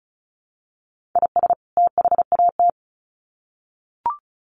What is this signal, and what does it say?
Morse code, heard as a keyed tone, about 34 wpm, 700 Hz. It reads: IS THAT
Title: Sets of 2 Words 35wpm